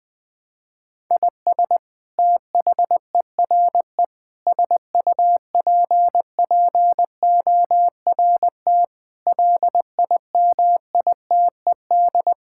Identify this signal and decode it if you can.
Morse code, heard as a keyed tone, about 20 wpm, 700 Hz. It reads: IS THERE SUPPORT LIMITED
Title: Sentences from Top 1000 Words - Review Part 6 - 20wpm